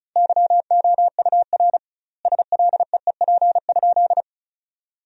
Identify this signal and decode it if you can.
Morse code, heard as a keyed tone, about 36 wpm, 700 Hz. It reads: YOUR SLEEP?
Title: 2x Word-Spacing - Sentences from Top 500 Words - 35wpm